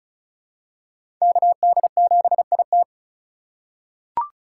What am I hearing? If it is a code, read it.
Morse code, heard as a keyed tone, about 34 wpm, 700 Hz. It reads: KD7IT
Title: USA Call Signs - 5 Characters - 4x Character Spacing - 35wpm